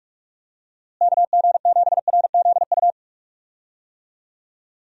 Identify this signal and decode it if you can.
Morse code, heard as a keyed tone, about 47 wpm, 700 Hz. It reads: KG7RZU
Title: USA Call Signs - 6 Characters - Repeated 3x - 45wpm